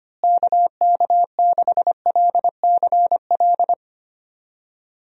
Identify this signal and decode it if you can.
Morse code, heard as a keyed tone, about 25 wpm, 700 Hz. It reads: KK6LCL
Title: USA Call Signs - 6 Characters - Repeated 2x - 25wpm